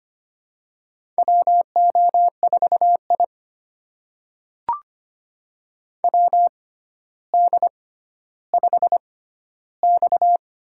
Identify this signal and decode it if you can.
Morse code, heard as a keyed tone, about 25 wpm, spaced as for 13 wpm, 700 Hz. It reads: WO4I WD5X
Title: USA Call Signs - 4 Characters - 6x Character Spacing - 25wpm